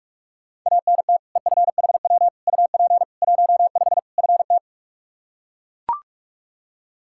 Morse code, heard as a keyed tone, about 42 wpm, 700 Hz. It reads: ANT EFHW UP 15 FT
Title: Practice QSO - Exchange 2 - Info - 45wpm